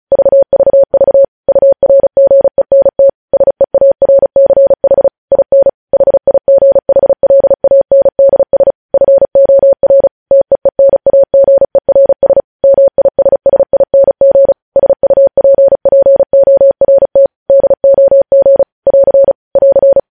Morse code, heard as a keyed tone, about 36 wpm, 550 Hz. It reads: VVV URGENT SEARCH IN HIGHLANDS FOR TEENAGERS MISSING SUPPORT DOG <AR> <AR>
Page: news headlines in morse code at 35 wpm